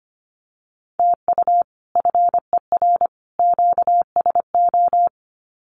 Morse code, heard as a keyed tone, about 25 wpm, 700 Hz. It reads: TU FER QSO
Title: Common QSO Phrases 25wpm